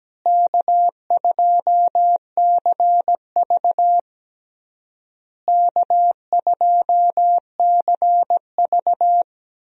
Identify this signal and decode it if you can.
Morse code, heard as a keyed tone, about 17 wpm, 700 Hz. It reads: K2CV K2CV
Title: USA Call Signs - 4 Characters - Repeated 3x - 17wpm